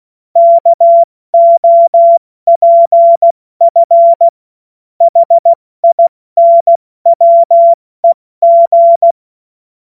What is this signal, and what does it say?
Morse code, heard as a keyed tone, about 16 wpm, 680 Hz. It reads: KOPF HINWEG
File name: Morsekurs / Witze / Witz 19 Dezember 2023 70 F80